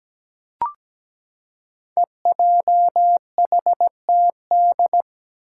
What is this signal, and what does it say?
Morse code, heard as a keyed tone, about 17 wpm, 700 Hz. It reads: EJHTD
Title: Five Challenging Letter-Number Combinations 17wpm